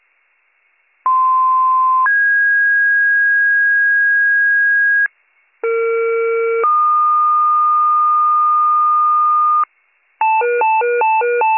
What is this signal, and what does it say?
Two-tone page, heard as a one-second tone, then a three-second tone, about 1040 then 1650 Hz, about 460 then 1220 Hz, with noise. It is heard from 1.1 to 9.6 s.